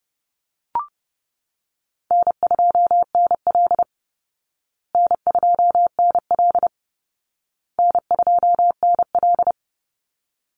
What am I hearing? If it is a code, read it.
Morse code, heard as a keyed tone, about 30 wpm, 700 Hz. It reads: N2NL N2NL N2NL
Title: Easy Call Signs Repeated 3x - 30wpm